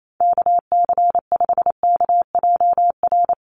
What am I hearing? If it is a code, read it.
Morse code, heard as a keyed tone, about 28 wpm, 700 Hz. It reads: KC5KJR
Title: USA Call Signs 28wpm